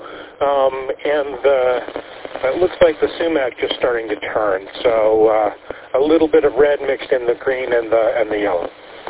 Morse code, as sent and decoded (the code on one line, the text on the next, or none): none